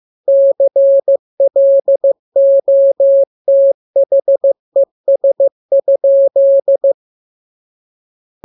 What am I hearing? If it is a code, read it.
Morse code, heard as a keyed tone, about 15 wpm, 550 Hz. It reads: CLOTHES?
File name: jokes / joke 19 15 0000